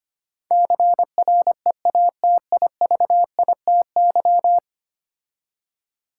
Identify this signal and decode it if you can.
Morse code, heard as a keyed tone, about 25 wpm, 700 Hz. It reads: CREATIVITY
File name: 10 Letter Words - First 3 Letters Given - 25wpm